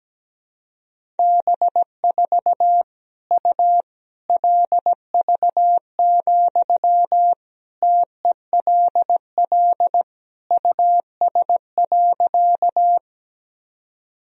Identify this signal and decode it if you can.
Morse code, heard as a keyed tone, about 17 wpm, 700 Hz. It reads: B4 U LV, TELL US.
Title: Sentences from Top 400 Words and Abbreviations 17wpm